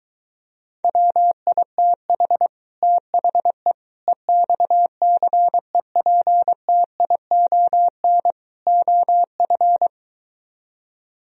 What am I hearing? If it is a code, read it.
Morse code, heard as a keyed tone, about 23 wpm, 700 Hz. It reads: WITH THE EXCEPTION OF